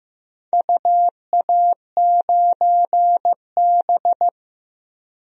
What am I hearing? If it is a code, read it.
Morse code, heard as a keyed tone, about 15 wpm, 700 Hz. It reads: UA9B
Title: DXCC-100 Call Signs - 4 Characters - 15wpm